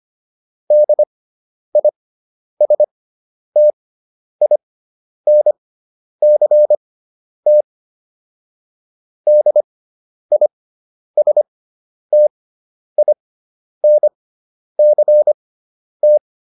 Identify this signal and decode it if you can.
Morse code, heard as a keyed tone, about 25 wpm, spaced as for 10 wpm, 600 Hz. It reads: DISTINCT DISTINCT